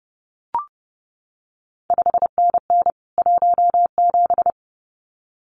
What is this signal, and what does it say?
Morse code, heard as a keyed tone, about 30 wpm, 700 Hz. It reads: 5NN 17
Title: CQ WW DX Contest Exchange Encore - 30wpm